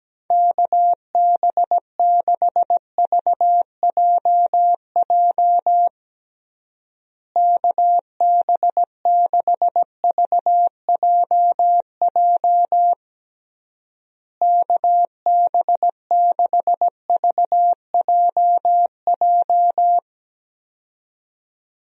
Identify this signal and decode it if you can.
Morse code, heard as a keyed tone, about 17 wpm, 700 Hz. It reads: KB6VJJ KB6VJJ KB6VJJ
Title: USA Call Signs - 6 Characters - Repeated 3x - 17wpm